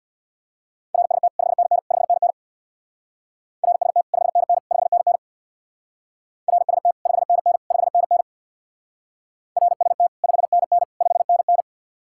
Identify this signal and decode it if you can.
Morse code, heard as a keyed tone, about 65 wpm, 700 Hz. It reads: RST 5NN 5NN RST 5NN 5NN RST 5NN 5NN RST 5NN 5NN